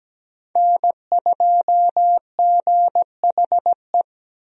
Morse code, heard as a keyed tone, about 17 wpm, 700 Hz. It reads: N2GHE
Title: USA Call Signs - 5 Characters - Repeated 3x - 17wpm